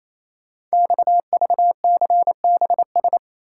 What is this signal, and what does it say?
Morse code, heard as a keyed tone, about 29 wpm, 700 Hz. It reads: XVCBS
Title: Code Groups - 5 Letters - 28wpm